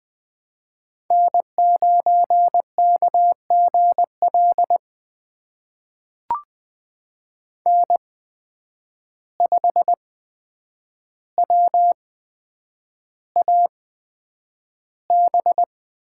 Morse code, heard as a keyed tone, about 20 wpm, spaced as for 9 wpm, 700 Hz. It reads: N9KGL N5WAB
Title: USA Call Signs - 5 Characters - 8x Character Spacing - 20wpm